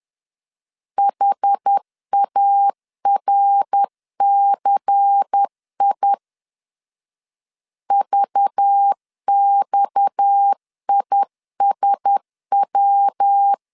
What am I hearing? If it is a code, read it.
Morse code, heard as a keyed tone, about 10 wpm, 800 Hz. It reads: HARCI VXISW
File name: LEZIONI / 6 PKX0 / 6 PKX 10wpm 2426